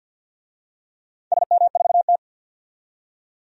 Morse code, heard as a keyed tone, about 50 wpm, 700 Hz. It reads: SM4T
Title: DXCC-100 Call Signs - 4 Characters - 50wpm